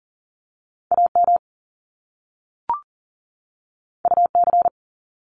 Morse code, heard as a keyed tone, about 40 wpm, 700 Hz. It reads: AM UC
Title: Top Two Letter Combinations in Words 40wpm